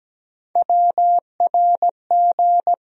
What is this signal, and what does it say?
Morse code, heard as a keyed tone, about 17 wpm, 700 Hz. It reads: WRG
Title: USA Call Sign Postfixes - 3 Letters 17wpm